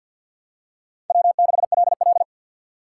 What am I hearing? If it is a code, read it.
Morse code, heard as a keyed tone, about 53 wpm, 700 Hz. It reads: W6LL
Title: USA Call Signs - 4 Characters - 2x Character Spacing - 50wpm